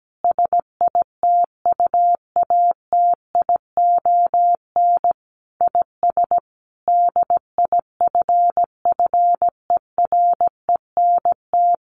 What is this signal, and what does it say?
Morse code, heard as a keyed tone, about 17 wpm, 700 Hz. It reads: SITUATION IS DIFFERENT
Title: Sentences from Top 1200 Words 17wpm